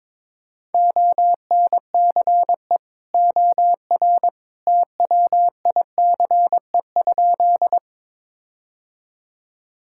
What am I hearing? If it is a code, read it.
Morse code, heard as a keyed tone, about 21 wpm, 700 Hz. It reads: ONCE OR TWICE?
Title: Sentences from Top 1600 Words 22wpm